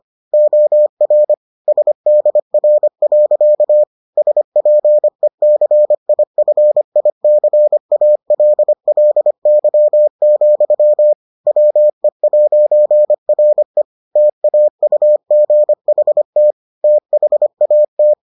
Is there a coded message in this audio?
Yes: Morse code from 0.3 to 18.1 s, heard as a keyed tone, about 25 wpm, 600 Hz. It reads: OR SDR. SPECIFICALLY, WE'RE TAUGHT THAT